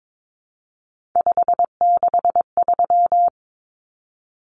Morse code, heard as a keyed tone, about 22 wpm, 700 Hz. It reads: 563
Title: Common Cut Numbers - 3-Digit 22wpm